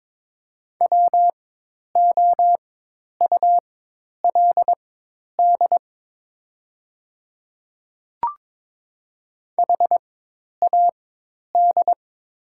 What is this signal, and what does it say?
Morse code, heard as a keyed tone, about 22 wpm, spaced as for 10 wpm, 700 Hz. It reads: WOULD HAD
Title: 4x Character Spacing - Top 100 Words - 22wpm